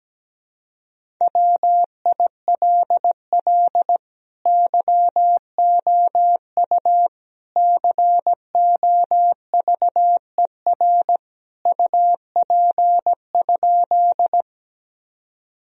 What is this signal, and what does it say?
Morse code, heard as a keyed tone, about 17 wpm, 700 Hz. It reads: WILL YOU COVER UP?
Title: Sentences from Top 1000 Words - Review Part 1 - 17wpm